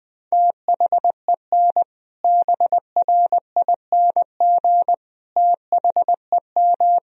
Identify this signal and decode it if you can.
Morse code, heard as a keyed tone, about 21 wpm, 700 Hz. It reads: THEN BRING THEM
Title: Sentences from Top 500 Words and Abbreviations 20wpm